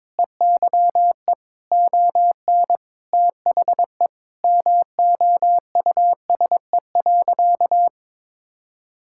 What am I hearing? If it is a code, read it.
Morse code, heard as a keyed tone, about 21 wpm, 690 Hz. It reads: EYE ON THE MOUSE.
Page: morse-code-ninja-practice-22wpm